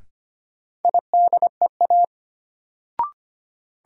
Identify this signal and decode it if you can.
Morse code, heard as a keyed tone, about 25 wpm, 700 Hz. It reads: IDEA
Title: Sets of 3 Words - New Word Drill - 25wpm